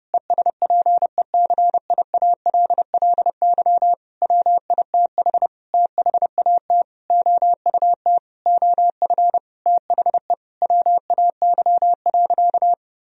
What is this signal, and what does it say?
Morse code, heard as a keyed tone, about 30 wpm, 700 Hz. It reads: ESPECIALLY WITH THAT OUT OF THE WAY.
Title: Sentences from Top 1000 Words - Review Part 4 - 30wpm